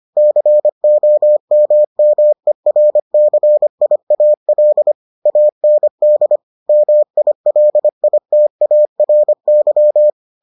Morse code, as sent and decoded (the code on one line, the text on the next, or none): -.-. --- -- -- . .-. -.-. .. .- .-.. / .- -. -.. / -- .. .-.. .. - .- .-. -.--
COMMERCIAL AND MILITARY